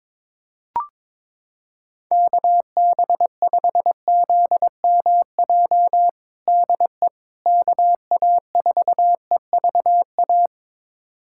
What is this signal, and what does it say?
Morse code, heard as a keyed tone, about 22 wpm, 700 Hz. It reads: KB5ZMJ DE KA4EVA
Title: USA Callsign DE Callsign - 6 Characters - 22wpm